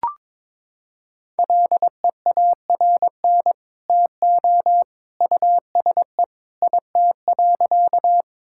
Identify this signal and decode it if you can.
Morse code, heard as a keyed tone, about 22 wpm, 700 Hz. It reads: LEARN TO USE IT.
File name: practice / 22wpm / Sentences with 2-Word Phrases - 22wpm